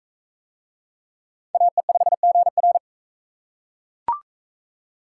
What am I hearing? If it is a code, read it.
Morse code, heard as a keyed tone, about 40 wpm, 700 Hz. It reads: AE5GR